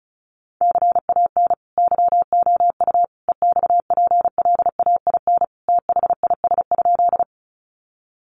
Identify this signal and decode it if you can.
Morse code, heard as a keyed tone, about 34 wpm, 700 Hz. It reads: CAN YOU EXPLAIN THIS?
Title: Sentences from Top 1100 Words 35wpm